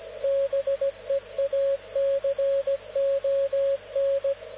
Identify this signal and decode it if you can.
Morse code, heard as a keyed tone, about 17 wpm, 550 Hz, with radio noise. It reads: BEACON
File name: luxorion / Radio / dk0wcy-spaceweather-10144khz